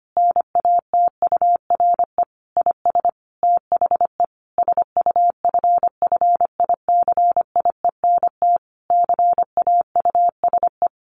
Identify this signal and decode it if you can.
Morse code, heard as a keyed tone, about 25 wpm, 700 Hz. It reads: NATURE IS THE SUFFICIENT CAUSE
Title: Sentences from Top 1400 Words 25wpm